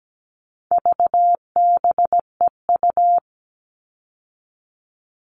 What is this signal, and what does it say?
Morse code, heard as a keyed tone, about 17 wpm, 700 Hz. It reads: VBEU
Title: All Cut Numbers - 4-Digit 17wpm